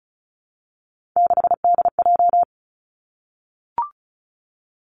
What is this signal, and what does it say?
Morse code, heard as a keyed tone, about 36 wpm, 700 Hz. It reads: 6DJ
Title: Three Challenging Letter-Number Combinations 35wpm